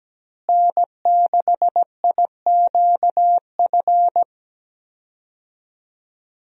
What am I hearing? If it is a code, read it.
Morse code, heard as a keyed tone, about 17 wpm, 700 Hz. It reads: N6IQF
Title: USA Call Signs - 5 Characters - Repeated 3x - 17wpm